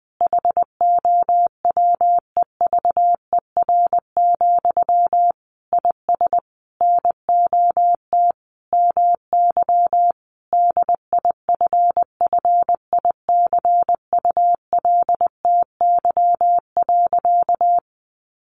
Morse code, heard as a keyed tone, about 20 wpm, 700 Hz. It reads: HOWEVER, IS NOT MY DIFFICULTY.
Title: Sentences from Top 1500 Words 20wpm